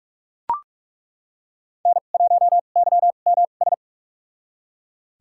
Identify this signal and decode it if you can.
Morse code, heard as a keyed tone, about 43 wpm, spaced as for 32 wpm, 700 Hz. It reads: N1YMS
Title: USA Call Signs - 2x Character Spacing - 45wpm